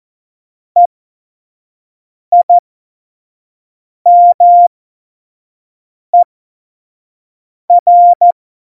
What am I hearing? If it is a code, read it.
Morse code, heard as a keyed tone, about 14 wpm, spaced as for 5 wpm, 700 Hz. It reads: EIMER
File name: Morsekurs / Witze / Witz 14 Juli 2020, 25 F70